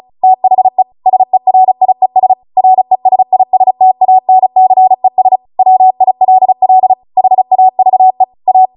Morse code, heard as a keyed tone, about 36 wpm, 750 Hz. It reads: THE SERIES RESISTANCES WILL HAVE A